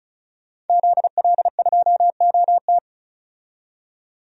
Morse code, heard as a keyed tone, about 36 wpm, 700 Hz. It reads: ZL2OT
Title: DXCC-100 Call Signs - 5 Characters - 35wpm